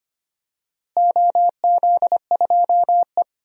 Morse code, heard as a keyed tone, about 25 wpm, 700 Hz. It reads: OZ2E
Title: DXCC-100 Call Signs - 4 Characters - Repeated 3x - 25wpm